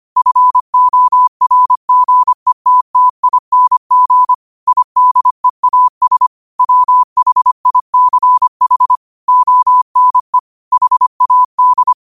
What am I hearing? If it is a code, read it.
Morse code, heard as a keyed tone, about 25 wpm, 1000 Hz. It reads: FORGETTING IDEAS WHICH ONE HAD
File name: qotdfiles / 25wpm